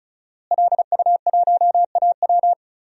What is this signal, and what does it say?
Morse code, heard as a keyed tone, about 36 wpm, 700 Hz. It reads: LU1AW
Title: DXCC-100 Call Signs - 5 Characters - Repeated 2x - 35wpm